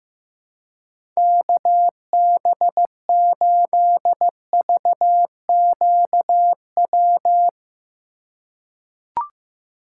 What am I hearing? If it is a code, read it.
Morse code, heard as a keyed tone, about 15 wpm, 700 Hz. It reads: KB8VQW
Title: USA Call Signs - 6 Characters - 4x Character Spacing - 15wpm